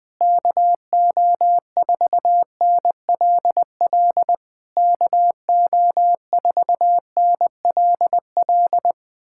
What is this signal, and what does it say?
Morse code, heard as a keyed tone, about 20 wpm, 700 Hz. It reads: KO4NLL KO4NLL
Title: POTA QSO Practice - 20wpm